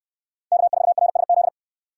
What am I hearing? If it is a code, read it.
Morse code, heard as a keyed tone, about 68 wpm, 700 Hz. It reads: K4RSZ